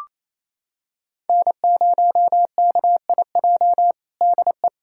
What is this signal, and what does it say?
Morse code, heard as a keyed tone, about 27 wpm, 700 Hz. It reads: N0KIJ DE
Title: USA Callsign DE Callsign - 5 Characters - 28wpm